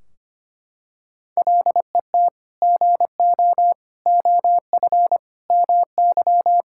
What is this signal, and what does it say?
Morse code, heard as a keyed tone, about 25 wpm, 700 Hz. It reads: LET GO OF MY
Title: Sentences from Top 1600 Words 25wpm